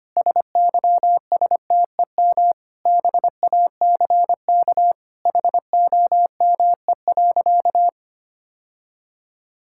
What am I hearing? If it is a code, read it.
Morse code, heard as a keyed tone, about 25 wpm, 700 Hz. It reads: SYSTEM BACK HOME.
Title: Sentences from Top 1000 Words - Review Part 5 - 25wpm